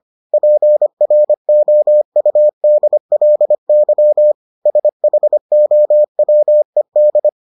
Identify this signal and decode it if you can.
Morse code, heard as a keyed tone, about 25 wpm, 600 Hz. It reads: PROUDLY SHOWED